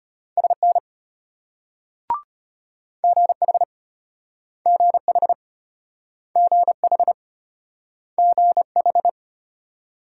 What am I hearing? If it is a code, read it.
Morse code, heard as a keyed tone, about 34 wpm, 700 Hz. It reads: SN GH GH GH GH